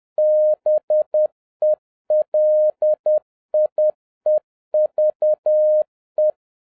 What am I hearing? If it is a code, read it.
Morse code, heard as a keyed tone, about 10 wpm, 610 Hz. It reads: BELIEVE